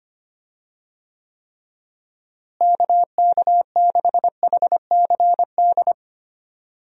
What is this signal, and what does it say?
Morse code, heard as a keyed tone, about 25 wpm, 700 Hz. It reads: KK6HCD
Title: USA Call Signs - 6 Characters - Repeated 3x - 25wpm